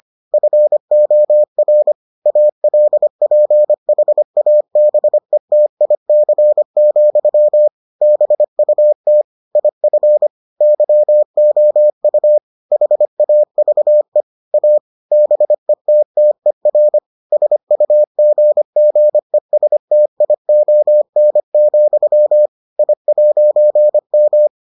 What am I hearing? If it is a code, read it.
Morse code, heard as a keyed tone, about 25 wpm, 600 Hz. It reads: FOR ALPHABETIC, BUT IF YOU HAVE A BETTER SUGGESTION, I'M